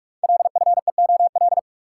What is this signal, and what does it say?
Morse code, heard as a keyed tone, about 44 wpm, 700 Hz. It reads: LFEOL
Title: Code Groups - 5 Letters - 45wpm